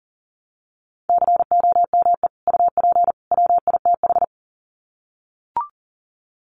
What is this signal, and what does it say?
Morse code, heard as a keyed tone, about 40 wpm, 700 Hz. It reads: COME UP WITH
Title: Sets of 3 Words 40wpm